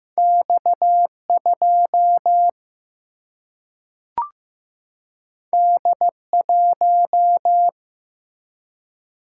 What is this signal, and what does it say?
Morse code, heard as a keyed tone, about 15 wpm, 700 Hz. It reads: X 2 D 1
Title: Single-Letter Single-Number 15wpm